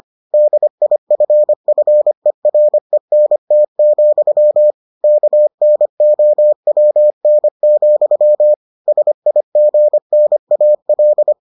Morse code, heard as a keyed tone, about 25 wpm, 600 Hz. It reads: DIFFERENT, KNOWN, SIGNAL